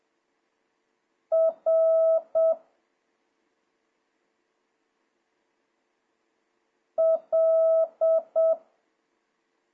Morse code, heard as a keyed tone, about 7 wpm, 1300 Hz. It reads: R L
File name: jh8cbh / 2026cw